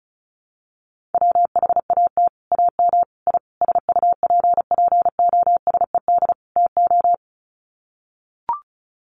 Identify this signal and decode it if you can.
Morse code, heard as a keyed tone, about 34 wpm, 700 Hz. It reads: WHAT AM I SUPPOSED TO
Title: Sets of 5 Words Encore2 35wpm